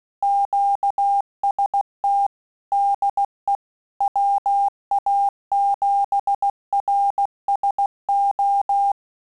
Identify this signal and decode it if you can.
Morse code, heard as a keyed tone, about 16 wpm, 800 Hz. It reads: QST DE WA7RSO